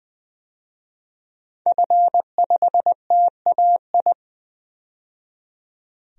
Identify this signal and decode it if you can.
Morse code, heard as a keyed tone, about 20 wpm, 700 Hz. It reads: F5TAI
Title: DXCC-100 Call Signs - 5 Characters - 20wpm